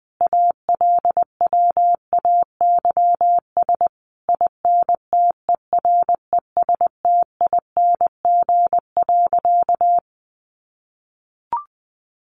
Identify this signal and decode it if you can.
Morse code, heard as a keyed tone, about 20 wpm, 700 Hz. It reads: ALWAYS INTERESTING.
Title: Sentences from Top 1200 Words 20wpm